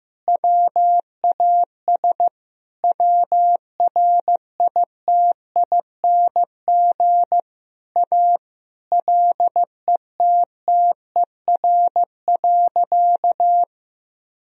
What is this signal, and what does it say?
Morse code, heard as a keyed tone, about 16 wpm, 700 Hz. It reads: WAS WRITING A LETTER.
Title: Sentences from Top 1000 Words - Review Part 4 - 15wpm